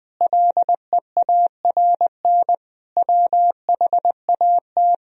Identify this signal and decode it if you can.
Morse code, heard as a keyed tone, about 20 wpm, 700 Hz. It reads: LEARN WHAT
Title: Sentences from Top 1000 Words - Review Part 3 - 20wpm